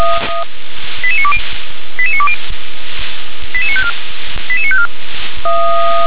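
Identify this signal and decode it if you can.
Five-tone selective call (ZVEI), heard as quick runs of five tones, tones of about 70 ms. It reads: 88920 88920 88954 88954
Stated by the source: ZVEI five-tone selective call 88953 received by SDR.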